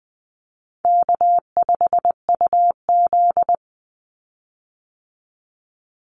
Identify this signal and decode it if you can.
Morse code, heard as a keyed tone, about 20 wpm, 700 Hz. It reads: K5UZ
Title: Easy Call Signs Repeated 3x - 20wpm